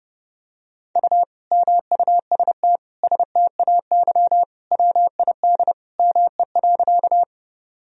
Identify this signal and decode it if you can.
Morse code, heard as a keyed tone, about 30 wpm, 700 Hz. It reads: U MUST STAY WID ME.
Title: Sentences from Top 500 Words and Abbreviations 30wpm